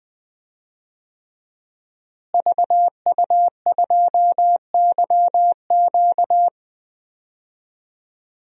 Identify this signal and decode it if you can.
Morse code, heard as a keyed tone, about 20 wpm, 700 Hz. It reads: VU2YQ
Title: DXCC-100 Call Signs - 5 Characters - Repeated 3x - 20wpm